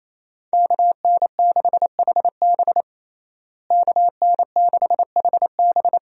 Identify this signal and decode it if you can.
Morse code, heard as a keyed tone, about 27 wpm, 700 Hz. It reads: KN6HB KN6HB
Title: USA Call Signs - 5 Characters - Repeated 3x - 28wpm